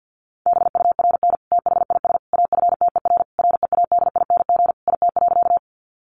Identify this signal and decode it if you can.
Morse code, heard as a keyed tone, about 50 wpm, 700 Hz. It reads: BURN THIS AFTER READING IT.
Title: Sentences from Top 1300 Words 50wpm